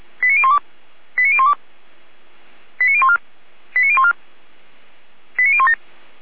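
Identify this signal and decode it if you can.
Five-tone selective call (ZVEI), heard as quick runs of five tones, tones of about 70 ms. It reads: 89012 89012 89014 89014 89017
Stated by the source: ZVEI five-tone call received by SDR, fire brigade Orlamünde.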